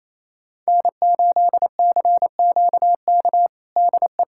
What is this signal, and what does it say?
Morse code, heard as a keyed tone, about 28 wpm, 700 Hz. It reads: N8CQK DE
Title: USA Callsign DE Callsign - 5 Characters - 28wpm